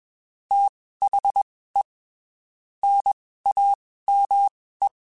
Morse code, heard as a keyed tone, about 21 wpm, spaced as for 15 wpm, 800 Hz. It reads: THE NAME
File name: (spaced) / QSOs Spaced-mp3s / WD0BGP 1 @20 sp